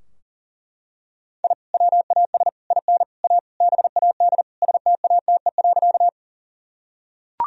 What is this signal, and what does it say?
Morse code, heard as a keyed tone, about 40 wpm, 700 Hz. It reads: I WAS IN A BAD STATE.
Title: Sentences from Top 1000 Words - Review Part 2 - 40wpm